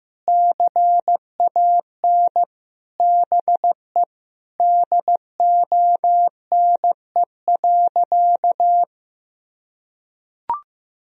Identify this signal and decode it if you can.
Morse code, heard as a keyed tone, about 15 wpm, 700 Hz. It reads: CAN BE DONE.
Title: Sentences from Top 1000 Words - Review Part 1 - 15wpm